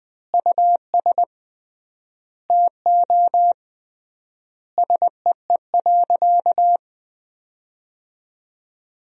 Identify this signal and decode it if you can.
Morse code, heard as a keyed tone, about 21 wpm, 700 Hz. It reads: US TO SEE.
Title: 3x Word-Spacing - Sentences from Top 500 Words - 20wpm